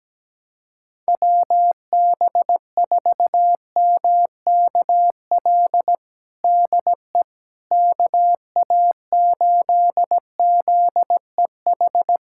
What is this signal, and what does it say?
Morse code, heard as a keyed tone, about 17 wpm, 700 Hz. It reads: WB4MKL DE KA8ZEH
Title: USA Callsign DE Callsign - 6 Characters - 17wpm